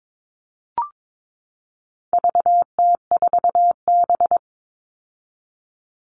Morse code, heard as a keyed tone, about 22 wpm, 700 Hz. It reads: VT4B
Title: All Cut Numbers - 4-Digit 22wpm